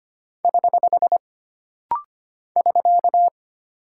Morse code, heard as a keyed tone, about 25 wpm, 700 Hz. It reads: <HH> <SK>
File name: Single Punctuation-Prosign - Rapid-Fire - 25wpm